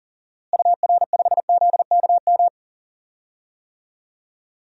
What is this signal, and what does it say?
Morse code, heard as a keyed tone, about 40 wpm, 700 Hz. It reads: UR5ZKM
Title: DXCC-100 Call Signs - 6 Characters - Repeated 3x - 40wpm